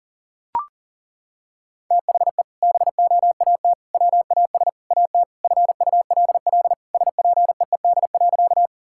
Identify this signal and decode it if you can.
Morse code, heard as a keyed tone, about 40 wpm, 700 Hz. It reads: THE BOAT WAS AT FULL SPEED.
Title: Sentences from Top 1000 Words - Review Part 5 - 40wpm